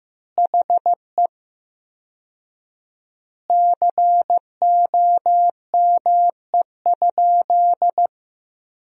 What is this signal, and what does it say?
Morse code, heard as a keyed tone, about 15 wpm, 700 Hz. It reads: HE COME?
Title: 4x Word-Spacing - Sentences from Top 200 Words - 15wpm